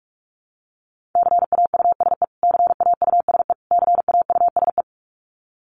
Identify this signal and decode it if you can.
Morse code, heard as a keyed tone, about 44 wpm, 700 Hz. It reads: CAUSE CAUSE CAUSE
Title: Top 200 Words Repeated 45wpm